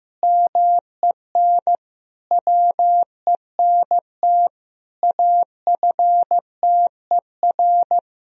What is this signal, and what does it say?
Morse code, heard as a keyed tone, about 15 wpm, 700 Hz. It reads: MEN WENT AFTER
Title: Sentences from Top 1000 Words - Review Part 2 - 15wpm